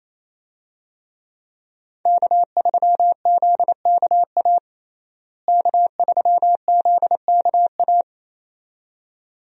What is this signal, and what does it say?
Morse code, heard as a keyed tone, about 27 wpm, 700 Hz. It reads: K3ZKA K3ZKA
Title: USA Call Signs - 5 Characters - Repeated 2x - 28wpm